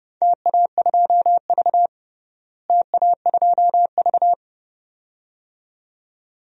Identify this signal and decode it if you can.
Morse code, heard as a keyed tone, about 30 wpm, 700 Hz. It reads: TA2V TA2V
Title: DXCC-100 Call Signs - 4 Characters - Repeated 2x - 30wpm